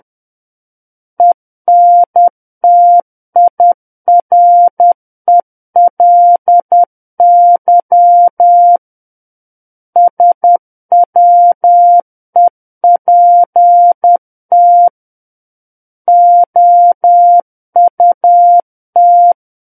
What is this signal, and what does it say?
Morse code, heard as a keyed tone, about 10 wpm, 700 Hz. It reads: ENTIRELY SWEPT OUT
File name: Book One / 01-Chapter 1 - 10 WPM